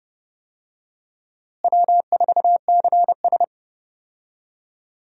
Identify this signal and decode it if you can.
Morse code, heard as a keyed tone, about 30 wpm, 700 Hz. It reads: W4CS
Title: USA Call Signs - 4 Characters - Repeated 2x - 30wpm